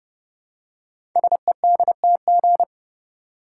Morse code, heard as a keyed tone, about 30 wpm, 700 Hz. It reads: SEDTG
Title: Code Groups - 5 Letter:Numbers - 30wpm